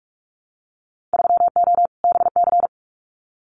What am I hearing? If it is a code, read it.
Morse code, heard as a keyed tone, about 44 wpm, 690 Hz. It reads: 3O BC
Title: Winter Field Day Exchanges Encore 45wpm